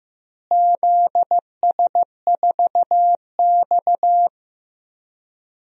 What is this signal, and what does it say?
Morse code, heard as a keyed tone, about 15 wpm, 700 Hz. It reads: ZS4X
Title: DXCC-100 Call Signs - 4 Characters - Repeated 2x - 15wpm